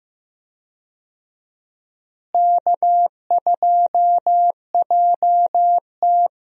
Morse code, heard as a keyed tone, about 15 wpm, 700 Hz. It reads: K2JT
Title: USA Call Signs - 4 Characters - Repeated 3x - 15wpm